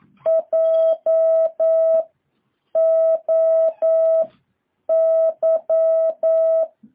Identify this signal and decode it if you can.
Morse code, heard as a keyed tone, about 9 wpm, 650 Hz. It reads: J O Y
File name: jh8cbh / 2023cw